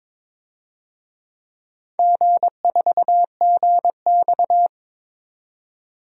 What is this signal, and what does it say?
Morse code, heard as a keyed tone, about 22 wpm, 700 Hz. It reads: G4GX
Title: DXCC-100 Call Signs - 4 Characters - 22wpm